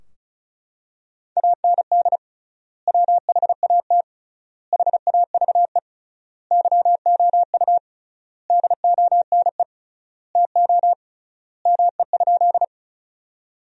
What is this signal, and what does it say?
Morse code, heard as a keyed tone, about 34 wpm, 700 Hz. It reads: AND WHAT HAVE YOU DONE TO ME?
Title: 3x Word-Spacing - Sentences from Top 500 Words - 35wpm